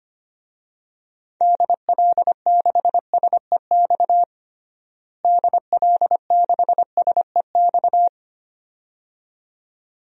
Morse code, heard as a keyed tone, about 25 wpm, 700 Hz. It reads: DL6SEX DL6SEX
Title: DXCC-100 Call Signs - 6 Characters - Repeated 2x - 25wpm